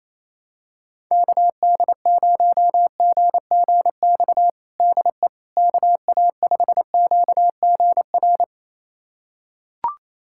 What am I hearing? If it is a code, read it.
Morse code, heard as a keyed tone, about 28 wpm, 700 Hz. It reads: KD0GGX DE KA5QGR
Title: USA Callsign DE Callsign 28wpm